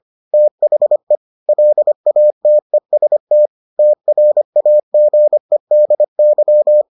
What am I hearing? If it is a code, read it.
Morse code, heard as a keyed tone, about 25 wpm, 600 Hz. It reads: THE LATEST TRAGEDY